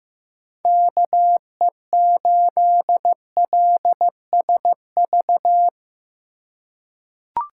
Morse code, heard as a keyed tone, about 15 wpm, 700 Hz. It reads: KE8LSV
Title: USA Call Signs - 4x Character Spacing - 15wpm